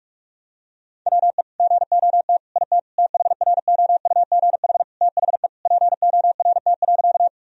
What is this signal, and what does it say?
Morse code, heard as a keyed tone, about 47 wpm, 700 Hz. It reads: WE GOT IT THROUGH THE PORT.
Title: Sentences from Top 1000 Words - Review Part 1 - 45wpm